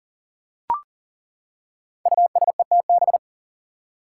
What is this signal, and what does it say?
Morse code, heard as a keyed tone, about 39 wpm, 700 Hz. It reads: USETB
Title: Five Challenging Letter-Number Combinations 40wpm